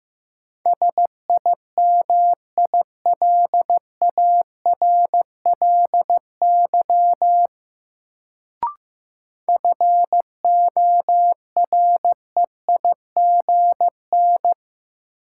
Morse code, heard as a keyed tone, about 15 wpm, 700 Hz. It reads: SIMILARLY FOREIGN
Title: Sentences from Top 1400 Words - New Word Drill - 15wpm